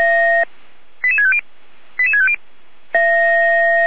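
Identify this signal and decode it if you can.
Five-tone selective call (ZVEI), heard as quick runs of five tones, tones of about 70 ms. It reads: 88650 88650
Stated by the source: ZVEI five-tone selective call received by SDR.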